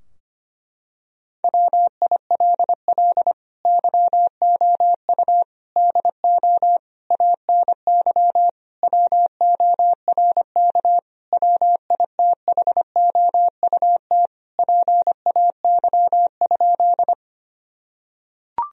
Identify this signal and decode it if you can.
Morse code, heard as a keyed tone, about 25 wpm, 700 Hz. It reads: WILL YOU DO ANY WORK WITHOUT PAY?
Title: Sentences from Top 1000 Words - Review Part 3 - 25wpm